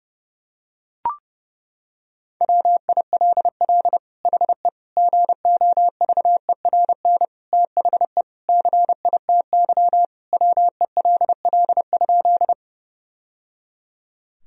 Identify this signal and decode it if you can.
Morse code, heard as a keyed tone, about 30 wpm, 700 Hz. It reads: WILL HE GOVERN THE CITY WELL?
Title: Sentences from Top 1000 Words - Review Part 2 - 30wpm